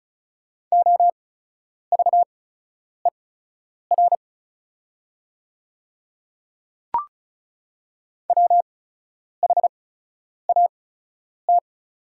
Morse code, heard as a keyed tone, about 36 wpm, spaced as for 10 wpm, 690 Hz. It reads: OVER WHAT
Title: 8x Character Spacing - Top 200 Words - 35wpm